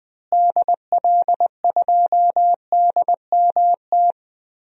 Morse code, heard as a keyed tone, about 20 wpm, 700 Hz. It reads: DL2DMT
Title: DXCC-100 Call Signs - 6 Characters - Repeated 2x - 20wpm